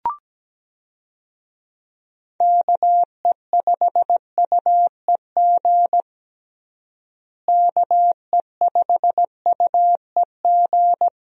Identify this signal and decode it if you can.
Morse code, heard as a keyed tone, about 17 wpm, 700 Hz. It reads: KE5UEG KE5UEG
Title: USA Call Signs - 6 Characters - Repeated 2x - 17wpm